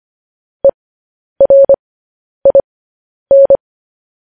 Morse code, heard as a keyed tone, about 25 wpm, spaced as for 10 wpm, 550 Hz. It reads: ERIN